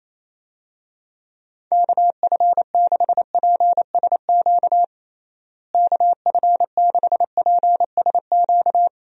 Morse code, heard as a keyed tone, about 28 wpm, 700 Hz. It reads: KF6PSQ KF6PSQ
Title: USA Call Signs - 6 Characters - Repeated 2x - 28wpm